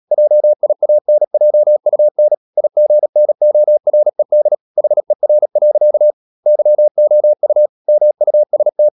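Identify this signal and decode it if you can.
Morse code, heard as a keyed tone, about 37 wpm, 600 Hz. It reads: JIANJUN IGNORED HER. YOU MUST
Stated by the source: Morse code practice at 35–39 wpm sending English text of Ken Follett's novel Never, from CW med Gnister.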